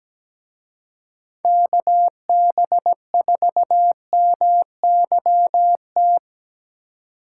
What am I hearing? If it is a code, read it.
Morse code, heard as a keyed tone, about 19 wpm, 700 Hz. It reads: KB4MYT
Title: Speed Racing - USA 2x3 Call Signs 17wpm